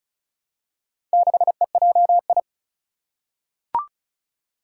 Morse code, heard as a keyed tone, about 36 wpm, 690 Hz. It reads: 6EJI
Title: Four Challenging Letter-Number Combinations 35wpm